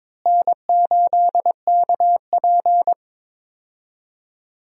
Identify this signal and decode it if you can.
Morse code, heard as a keyed tone, about 22 wpm, 700 Hz. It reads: N8KP
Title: USA Call Signs - 4 Characters - Repeated 2x - 22wpm